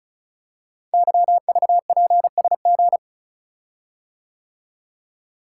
Morse code, heard as a keyed tone, about 36 wpm, 700 Hz. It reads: YVPSG
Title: Code Groups - 5 Letters - 35wpm